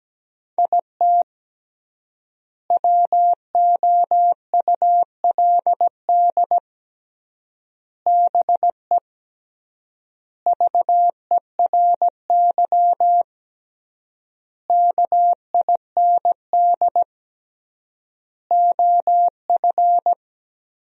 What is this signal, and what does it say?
Morse code, heard as a keyed tone, about 17 wpm, 700 Hz. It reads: IT WOULD BE VERY KIND OF
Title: 3x Word-Spacing - Sentences from Top 200 Words - 17wpm